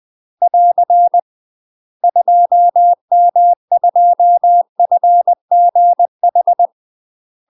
Morse code, heard as a keyed tone, about 20 wpm, 700 Hz. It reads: <AR> 2M2FGH
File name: UK MASTER QSOs TRAINING / 20WPM / UK MASTER QSOs 0068